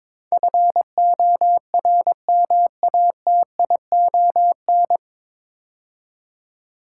Morse code, heard as a keyed tone, about 22 wpm, 700 Hz. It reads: FORMATION
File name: 9 Letter Words - First 3 Letters Given - 22wpm